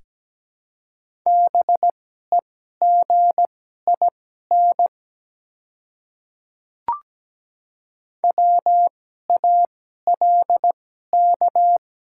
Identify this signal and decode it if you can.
Morse code, heard as a keyed tone, about 17 wpm, spaced as for 12 wpm, 700 Hz. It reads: BEGIN WALK
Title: 2x Character Spacing - Top 300 Words - 17wpm